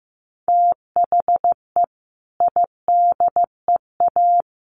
Morse code, heard as a keyed tone, about 15 wpm, 700 Hz. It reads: THE IDEA
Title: Sets of 3 Words 15wpm